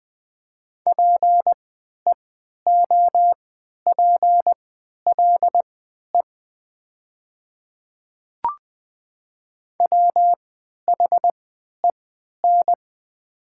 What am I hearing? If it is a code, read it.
Morse code, heard as a keyed tone, about 20 wpm, spaced as for 11 wpm, 700 Hz. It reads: PEOPLE WHEN
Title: 3x Character Spacing - Top 100 Words - 20wpm